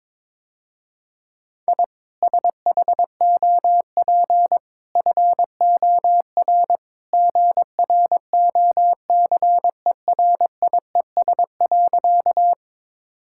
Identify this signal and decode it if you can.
Morse code, heard as a keyed tone, about 22 wpm, 700 Hz. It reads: I SHOP FOR GROCERIES.
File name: practice / 22wpm / Easy Sentences with Hints - 22wpm